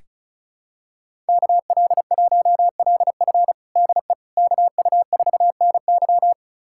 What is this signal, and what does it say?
Morse code, heard as a keyed tone, about 34 wpm, 700 Hz. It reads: KL1LF DE KU4NY
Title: USA Callsign DE Callsign - 5 Characters - 35wpm